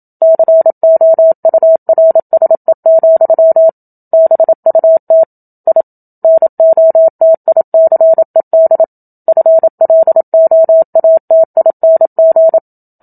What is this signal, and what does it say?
Morse code, heard as a keyed tone, about 27 wpm, 650 Hz. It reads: COURSE, BUT I NOTICED FLOATING
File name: Book Two / 01-Chapter 1 - 27 WPM